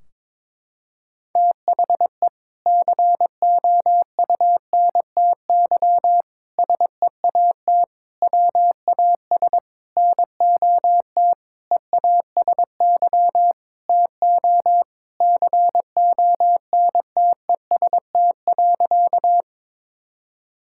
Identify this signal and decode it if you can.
Morse code, heard as a keyed tone, about 23 wpm, 700 Hz. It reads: THE COUNTY SEAT WAS NOT EASY TO CONTEST.
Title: Sentences from Top 1600 Words 22wpm